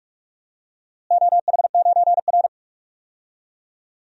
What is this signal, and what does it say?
Morse code, heard as a keyed tone, about 46 wpm, 700 Hz. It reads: OH9R
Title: DXCC-100 Call Signs - 4 Characters - Repeated 3x - 45wpm